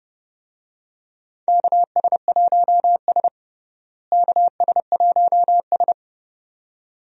Morse code, heard as a keyed tone, about 30 wpm, 700 Hz. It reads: KS1S KS1S
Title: USA Call Signs - 4 Characters - Repeated 2x - 30wpm